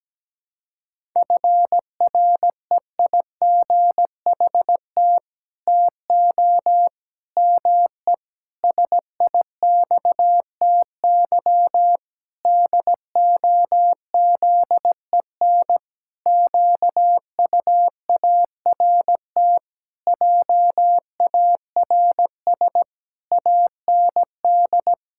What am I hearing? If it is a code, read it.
Morse code, heard as a keyed tone, about 17 wpm, 700 Hz. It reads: FREIGHT TO ME SIXTY DOZEN QUART JARS AND
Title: Pangram Challenge v2 - 17wpm